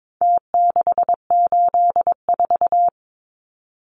Morse code, heard as a keyed tone, about 22 wpm, 700 Hz. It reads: T684
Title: Numbers - Contest Style - 4-Digit 22wpm